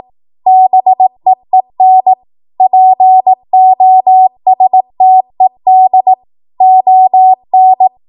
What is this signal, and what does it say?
Morse code, heard as a keyed tone, about 18 wpm, 750 Hz. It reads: BEEN POSTED ON